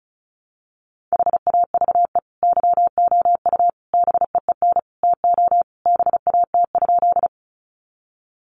Morse code, heard as a keyed tone, about 34 wpm, 700 Hz. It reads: HAVE YOU BEEN TO BAT?
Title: Sentences from Top 1500 Words 35wpm